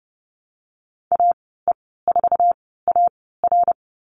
Morse code, heard as a keyed tone, about 30 wpm, spaced as for 17 wpm, 700 Hz. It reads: AE4AR
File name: Lesson 14g - Number 2 - Call Signs 30wpm